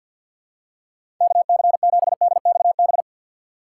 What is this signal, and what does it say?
Morse code, heard as a keyed tone, about 52 wpm, 700 Hz. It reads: KC7DXB
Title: USA Call Signs - 3x Character Spacing - 50wpm